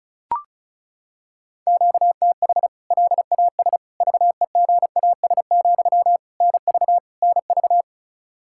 Morse code, heard as a keyed tone, about 34 wpm, 700 Hz. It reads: QTH LAS VEGAS, NV NV
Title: K1USN SST QSO Practice 35wpm